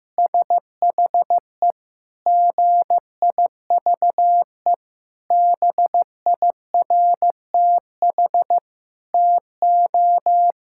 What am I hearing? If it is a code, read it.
Morse code, heard as a keyed tone, about 15 wpm, 700 Hz. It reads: SHE GIVE BIRTH TO
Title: Sentences from Top 1000 Words - Review Part 5 - 15wpm